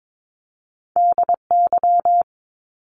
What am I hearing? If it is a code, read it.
Morse code, heard as a keyed tone, about 22 wpm, 700 Hz. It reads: DY
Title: Top Two Letter Combinations in Words 22wpm